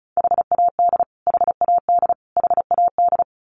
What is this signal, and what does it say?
Morse code, heard as a keyed tone, about 34 wpm, 700 Hz. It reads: HAD HAD HAD
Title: Top 500 Words Repeated 35wpm